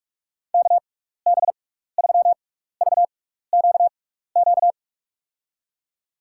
Morse code, heard as a keyed tone, about 46 wpm, spaced as for 16 wpm, 690 Hz. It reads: KB3VQQ
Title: USA Call Signs - 6x Character Spacing - 45wpm